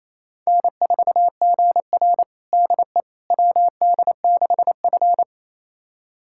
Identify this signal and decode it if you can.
Morse code, heard as a keyed tone, about 28 wpm, 700 Hz. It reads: N4GR DE WD6F
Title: USA Callsign DE Callsign - 4 Characters - 28wpm